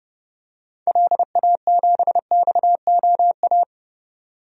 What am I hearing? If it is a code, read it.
Morse code, heard as a keyed tone, about 32 wpm, 700 Hz. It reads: LA7XOA
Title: DXCC-100 Call Signs - 6 Characters - 30wpm